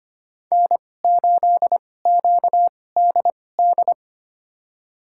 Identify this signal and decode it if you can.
Morse code, heard as a keyed tone, about 25 wpm, spaced as for 18 wpm, 700 Hz. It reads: N8QDD
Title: USA Call Signs - 2x Character Spacing - 25wpm